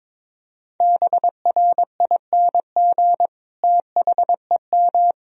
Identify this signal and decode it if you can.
Morse code, heard as a keyed tone, about 22 wpm, 700 Hz. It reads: BRING THEM
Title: Sentences from Top 500 Words and Abbreviations 22wpm